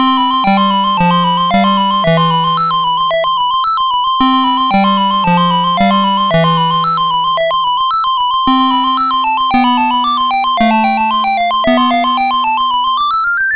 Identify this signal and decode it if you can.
There is no code.